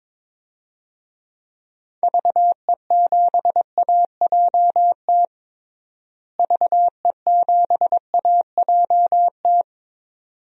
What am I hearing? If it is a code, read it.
Morse code, heard as a keyed tone, about 21 wpm, 700 Hz. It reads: VE7AJT VE7AJT
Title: DXCC-100 Call Signs - 6 Characters - Repeated 2x - 22wpm